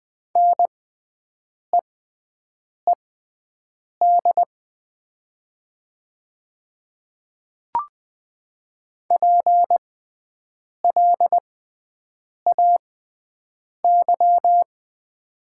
Morse code, heard as a keyed tone, about 20 wpm, spaced as for 7 wpm, 700 Hz. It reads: NEED PLAY
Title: 6x Character Spacing - Top 200 Words - 20wpm